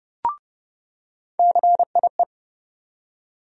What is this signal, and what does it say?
Morse code, heard as a keyed tone, about 30 wpm, 700 Hz. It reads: CIE
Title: USA Call Sign Postfixes - 3 Letters 30wpm